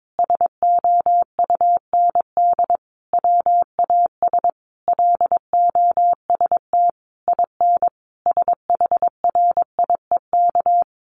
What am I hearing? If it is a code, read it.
Morse code, heard as a keyed tone, about 22 wpm, 700 Hz. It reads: SOUND WAS LOST IN SHRIEK
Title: A Sound Of Thunder 22wpm